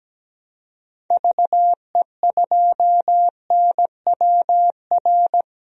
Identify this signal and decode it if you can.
Morse code, heard as a keyed tone, about 17 wpm, 700 Hz. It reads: VE2NWR
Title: DXCC-100 Call Signs - 6 Characters - 17wpm